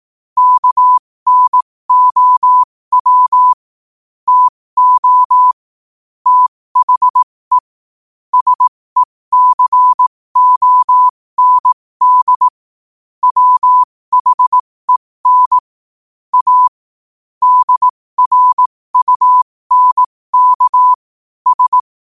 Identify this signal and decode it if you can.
Morse code, heard as a keyed tone, about 18 wpm, spaced as for 15 wpm, 1000 Hz. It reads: KNOW TO THE SECOND WHEN A DRUNK S
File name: qotdfiles / 15wpm